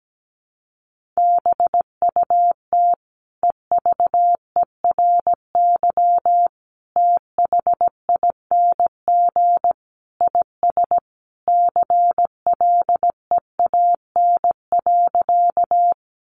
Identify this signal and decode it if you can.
Morse code, heard as a keyed tone, about 17 wpm, 700 Hz. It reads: BUT EVERY THING IS CLEAN.
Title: Sentences from Top 1100 Words 17wpm